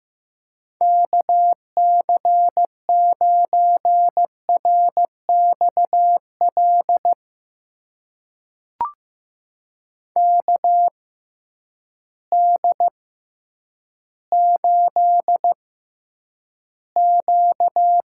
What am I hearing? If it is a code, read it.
Morse code, heard as a keyed tone, about 15 wpm, spaced as for 9 wpm, 700 Hz. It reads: KC9RXL KD8Q
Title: USA Call Signs - 6 Characters - 6x Character Spacing - 15wpm